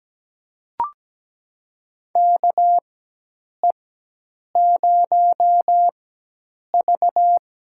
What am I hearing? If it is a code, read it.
Morse code, heard as a keyed tone, about 17 wpm, spaced as for 8 wpm, 700 Hz. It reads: KE0V
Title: USA Call Signs - 4x Character Spacing - 17wpm